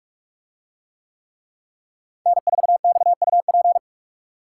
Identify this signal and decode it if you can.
Morse code, heard as a keyed tone, about 44 wpm, 700 Hz. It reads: N4XUP